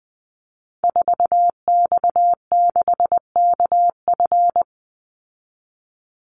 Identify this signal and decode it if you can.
Morse code, heard as a keyed tone, about 20 wpm, 700 Hz. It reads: 4X6KF
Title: Call Signs Repeated 3x - 20wpm